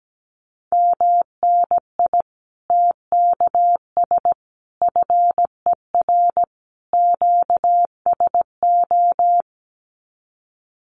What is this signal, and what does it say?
Morse code, heard as a keyed tone, about 17 wpm, 700 Hz. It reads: MNI TKS FER QSO
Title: Common QSO Phrases 17wpm